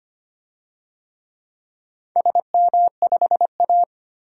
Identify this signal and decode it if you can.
Morse code, heard as a keyed tone, about 25 wpm, 700 Hz. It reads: SM5A
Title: DXCC-100 Call Signs - 4 Characters - 25wpm